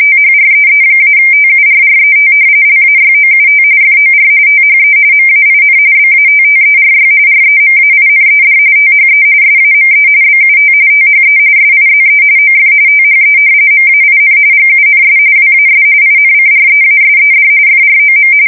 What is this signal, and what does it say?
RTTY, about 45.45 baud. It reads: THREE-TURRET LAYOUT FOR THE MAIN BATTERY REMAINED THE SAME BUT THE SIZE OF THE MAIN GUNS WAS INCREASED FROM
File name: fsk170 / battleshipnj